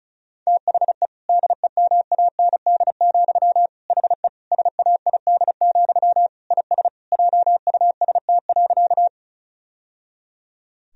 Morse code, heard as a keyed tone, about 36 wpm, 700 Hz. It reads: THE DEMAND, HE SAID, IS JUST.
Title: Sentences from Top 1000 Words - Review Part 4 - 35wpm